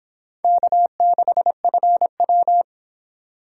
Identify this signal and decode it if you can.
Morse code, heard as a keyed tone, about 25 wpm, 700 Hz. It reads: K6FW